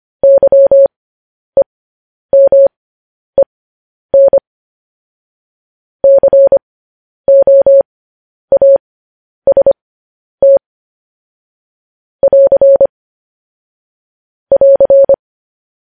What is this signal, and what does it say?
Morse code, heard as a keyed tone, about 25 wpm, spaced as for 10 wpm, 550 Hz. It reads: YEMEN COAST <AR> <AR>